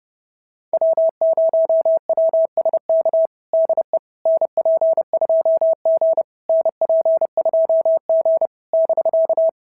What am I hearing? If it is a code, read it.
Morse code, heard as a keyed tone, about 28 wpm, 650 Hz. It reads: W0WSK DE NP2G NP2G <BK>
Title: SOTA QSO Practice 30wpm